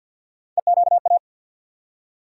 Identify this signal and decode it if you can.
Morse code, heard as a keyed tone, about 48 wpm, 700 Hz. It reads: EQA
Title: USA Call Sign Postfixes - 3 Letters 50wpm